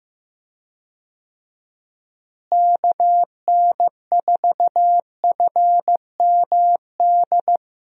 Morse code, heard as a keyed tone, about 15 wpm, 700 Hz. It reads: KN4FMD
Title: USA Call Signs - 6 Characters - Repeated 3x - 15wpm